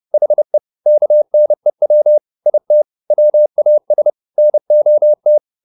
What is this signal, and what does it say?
Morse code, heard as a keyed tone, about 30 wpm, 600 Hz. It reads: HE KNEW IT WAS NOT
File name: the old man and the sea / the old man and the sea 1 30wpm